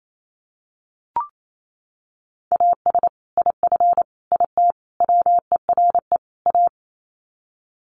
Morse code, heard as a keyed tone, about 28 wpm, 700 Hz. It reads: AS IF IT WERE A
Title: Sets of 5 Words 28wpm